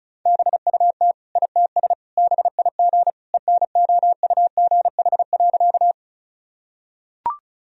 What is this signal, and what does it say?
Morse code, heard as a keyed tone, about 36 wpm, 700 Hz. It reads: BUT ITS BIG ENOUGH.
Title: Sentences from Top 1000 Words - Review Part 1 - 35wpm